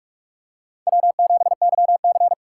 Morse code, heard as a keyed tone, about 46 wpm, 700 Hz. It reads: W7YC
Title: USA Call Signs - 2x Character Spacing - 45wpm